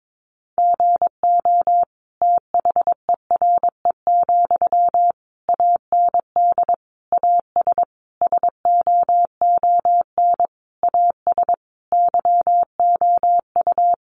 Morse code, heard as a keyed tone, about 22 wpm, 700 Hz. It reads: GO THERE, AND AS SOON AS YOU
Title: Sentences from Top 1300 Words 22wpm